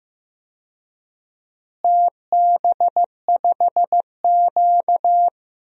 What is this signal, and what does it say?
Morse code, heard as a keyed tone, about 16 wpm, 700 Hz. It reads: TB5Q
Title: DXCC-100 Call Signs - 4 Characters - Repeated 3x - 15wpm